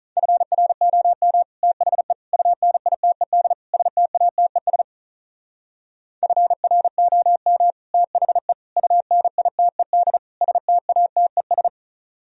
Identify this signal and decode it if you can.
Morse code, heard as a keyed tone, about 44 wpm, 700 Hz. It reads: FROM THE UNITED STATES FROM THE UNITED STATES